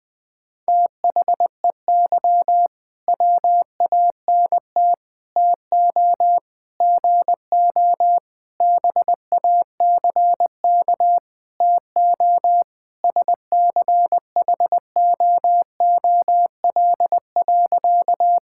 Morse code, heard as a keyed tone, about 21 wpm, 700 Hz. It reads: THEY WANT TO GO BACK TO SCHOOL.
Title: Sentences with 5-Word Phrases - 20wpm